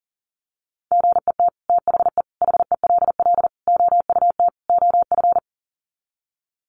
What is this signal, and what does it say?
Morse code, heard as a keyed tone, about 40 wpm, 700 Hz. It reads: GET THE HELL OUT OF
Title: Sets of 5 Words Encore2 40wpm